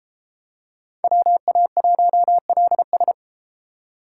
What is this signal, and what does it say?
Morse code, heard as a keyed tone, about 33 wpm, 700 Hz. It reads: WA1LS